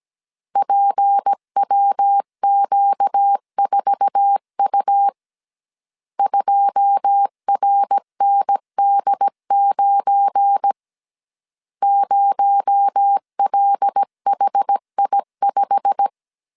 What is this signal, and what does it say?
Morse code, heard as a keyed tone, about 17 wpm, 800 Hz. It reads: PWQ4U 2RND9 0LHI5
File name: CW / DET / 1 9 1500C S A 16wpm 2828